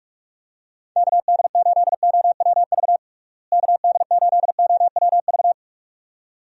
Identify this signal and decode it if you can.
Morse code, heard as a keyed tone, about 42 wpm, 700 Hz. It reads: KD8OWV KD8OWV
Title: USA Call Signs - 6 Characters - Repeated 2x - 45wpm